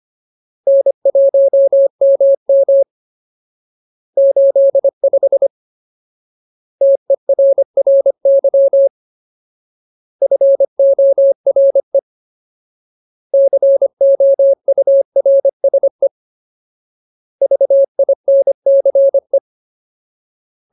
Morse code, heard as a keyed tone, about 25 wpm, 550 Hz. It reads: N1MM 85 TERRY FORE COURSE VINCE
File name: Ultimate Word List / 25WPM Ultimate